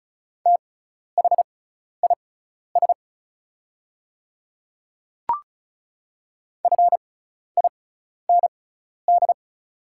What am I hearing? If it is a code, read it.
Morse code, heard as a keyed tone, about 34 wpm, spaced as for 12 wpm, 700 Hz. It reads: THIS FIND
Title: 6x Character Spacing - Top 100 Words - 35wpm